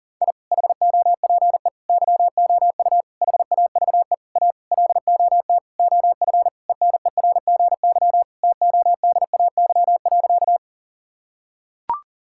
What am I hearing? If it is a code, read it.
Morse code, heard as a keyed tone, about 40 wpm, 700 Hz. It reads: I HOPE YOU HAVE A LOT OF ENERGY TODAY.
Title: Sentences from Top 1000 Words - Review Part 6 - 40wpm